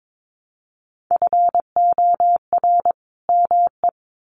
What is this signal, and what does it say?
Morse code, heard as a keyed tone, about 22 wpm, 700 Hz. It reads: FOR ME
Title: Sets of 2 Words 22wpm